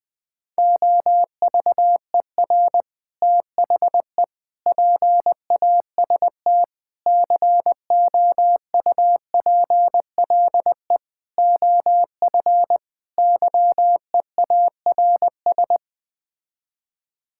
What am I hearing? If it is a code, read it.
Morse code, heard as a keyed tone, about 20 wpm, 700 Hz. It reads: OVER THE PAST COUPLE OF YEARS
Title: Sets of 6 Words - 20wpm